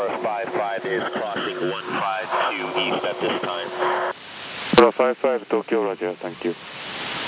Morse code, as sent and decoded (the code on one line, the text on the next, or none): none